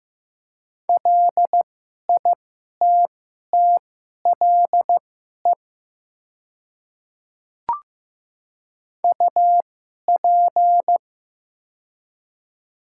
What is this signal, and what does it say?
Morse code, heard as a keyed tone, about 15 wpm, spaced as for 11 wpm, 700 Hz. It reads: LITTLE UP
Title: 2x Character Spacing - Top 200 Words - 15wpm